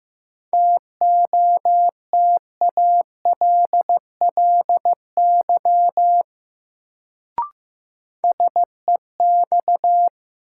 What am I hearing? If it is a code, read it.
Morse code, heard as a keyed tone, about 15 wpm, 700 Hz. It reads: TOTALLY SEX
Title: Sentences from Top 1600 Words - New Word Drill - 15wpm